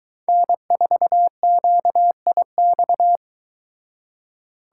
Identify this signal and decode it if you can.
Morse code, heard as a keyed tone, about 23 wpm, 700 Hz. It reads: N4QIX